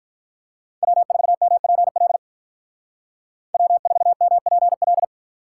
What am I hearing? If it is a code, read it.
Morse code, heard as a keyed tone, about 46 wpm, 700 Hz. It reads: W4MPL W4MPL